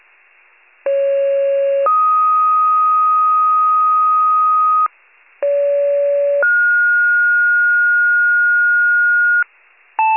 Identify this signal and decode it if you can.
Two-tone page, heard as a one-second tone, then a three-second tone, about 550 then 1250 Hz, about 570 then 1430 Hz, with noise. It is heard from 0.9 to 9.4 s.